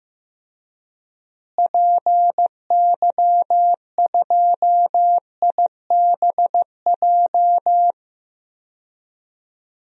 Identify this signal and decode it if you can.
Morse code, heard as a keyed tone, about 15 wpm, 700 Hz. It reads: PY2IBJ
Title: DXCC-100 Call Signs - 6 Characters - Repeated 3x - 15wpm